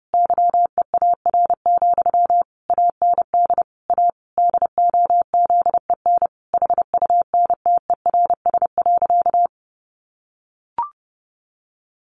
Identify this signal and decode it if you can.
Morse code, heard as a keyed tone, about 30 wpm, 700 Hz. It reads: YEAR, AND A DOZEN HUNTERS.
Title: A Sound Of Thunder 30wpm